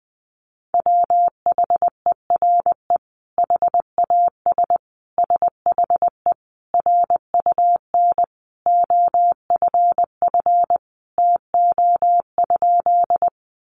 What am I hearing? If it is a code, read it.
Morse code, heard as a keyed tone, about 19 wpm, 700 Hz. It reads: WHERE HAS SHE RUN OFF TO?
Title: Sentences from Top 300 Words 20wpm